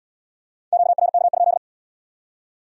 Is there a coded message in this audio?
Yes: Morse code from 0.7 to 1.6 s, heard as a keyed tone, about 74 wpm, 700 Hz. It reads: QRK?